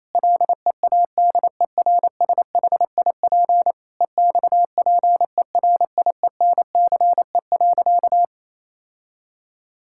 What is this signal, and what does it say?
Morse code, heard as a keyed tone, about 28 wpm, 700 Hz. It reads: LEADERSHIP EXPERIENCE.
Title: Sentences from Top 1600 Words 28wpm